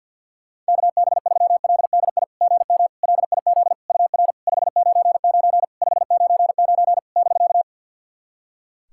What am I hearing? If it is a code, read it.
Morse code, heard as a keyed tone, about 50 wpm, 690 Hz. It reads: KB3LDI GM LIZ UR 599 599 <BK>
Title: SOTA QSO Practice 50wpm